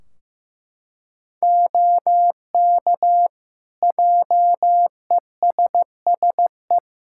Morse code, heard as a keyed tone, about 15 wpm, 700 Hz. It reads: OK JESSE
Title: Practice QSO - Exchange 2 - Info - 15wpm